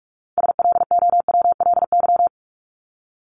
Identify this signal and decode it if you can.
Morse code, heard as a keyed tone, about 44 wpm, 700 Hz. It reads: SLOWLY
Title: Top 1001-1250 Words 45wpm